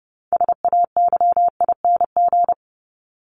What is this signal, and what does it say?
Morse code, heard as a keyed tone, about 30 wpm, 700 Hz. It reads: SAYING
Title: ING Suffix - Top 100 Words with Base Words 30wpm